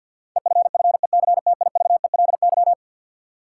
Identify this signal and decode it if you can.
Morse code, heard as a keyed tone, about 52 wpm, 700 Hz. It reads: EFFECTIVELY
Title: Sentences from Top 1400 Words - New Word Drill - 50wpm